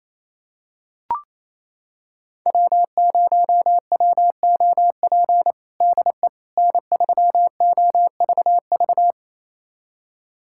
Morse code, heard as a keyed tone, about 27 wpm, 700 Hz. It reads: W0WOP DE N3OVV
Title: USA Callsign DE Callsign - 5 Characters - 28wpm